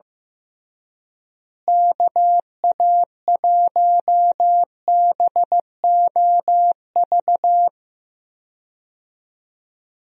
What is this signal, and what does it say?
Morse code, heard as a keyed tone, about 15 wpm, 700 Hz. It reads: KA1BOV
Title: USA Call Signs - 6 Characters - Repeated 3x - 15wpm